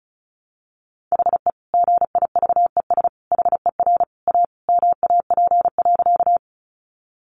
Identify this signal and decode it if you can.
Morse code, heard as a keyed tone, about 34 wpm, 700 Hz. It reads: HE GIVES HER A MAP.
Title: Additional Easy Phrases 35wpm